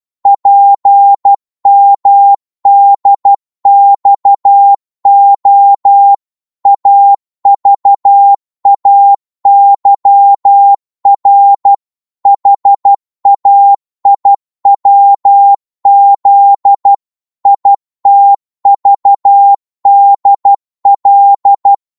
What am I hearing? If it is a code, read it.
Morse code, heard as a keyed tone, about 12 wpm, 800 Hz. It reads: PMDXO AVAYR HAIWZ ITVDL
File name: Morse / ltr-12wpm